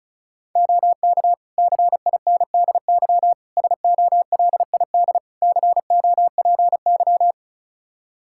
Morse code, heard as a keyed tone, about 36 wpm, 700 Hz. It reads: OK CINDY SOLID COPY
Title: K1USN SST QSO Practice 35wpm